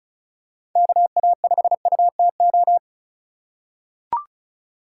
Morse code, heard as a keyed tone, about 34 wpm, 700 Hz. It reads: KA5UTO
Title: USA Call Signs - 6 Characters - 3x Character Spacing - 35wpm